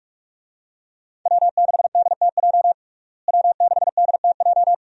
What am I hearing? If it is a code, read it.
Morse code, heard as a keyed tone, about 46 wpm, 700 Hz. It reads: W6DTJ W6DTJ
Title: USA Call Signs - 5 Characters - Repeated 2x - 45wpm